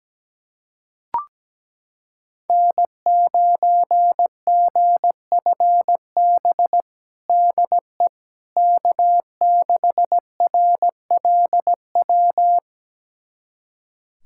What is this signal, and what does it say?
Morse code, heard as a keyed tone, about 17 wpm, 700 Hz. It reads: N9GFB DE K6RLW
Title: USA Callsign DE Callsign - 5 Characters - 17wpm